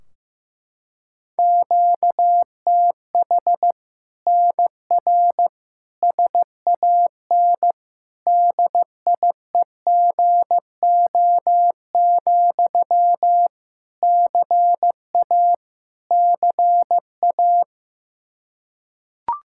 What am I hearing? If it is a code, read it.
Morse code, heard as a keyed tone, about 15 wpm, 700 Hz. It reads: QTH NR SAN DIEGO, CA CA
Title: Practice QSO - Exchange 1 - Beginning - 15wpm